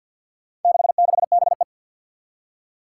Morse code, heard as a keyed tone, about 51 wpm, 700 Hz. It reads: 66BE
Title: Four Challenging Letter-Number Combinations 50wpm